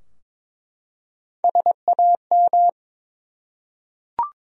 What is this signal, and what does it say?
Morse code, heard as a keyed tone, about 21 wpm, 700 Hz. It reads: SAM
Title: USA Common Names - Max 4 Letters - 22wpm